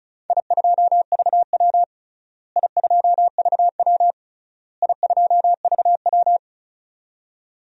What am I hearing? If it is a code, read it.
Morse code, heard as a keyed tone, about 36 wpm, 700 Hz. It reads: I2VW I2VW I2VW
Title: DXCC-100 Call Signs - 4 Characters - Repeated 3x - 35wpm